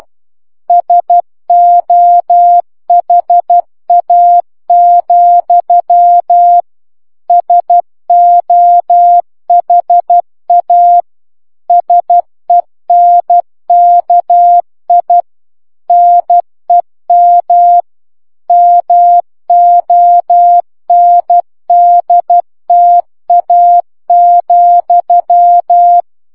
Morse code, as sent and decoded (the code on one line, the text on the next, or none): ... --- .... .- --..-- / ... --- .... .- / ... -. -.- .. / -. -- / -- --- -. -.. - .- --..--
SOHA, SOHA SENKI NEM MONDTA,